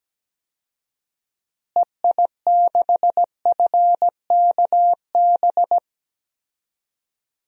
Morse code, heard as a keyed tone, about 17 wpm, 700 Hz. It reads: EI6FKB
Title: DXCC-100 Call Signs - 6 Characters - Repeated 2x - 17wpm